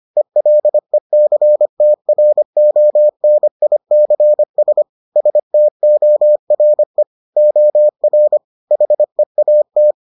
Morse code, heard as a keyed tone, about 25 wpm, 600 Hz. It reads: ELECTRONICS STORE OR HEAT